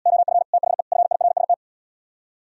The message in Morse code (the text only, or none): OF THE FIRST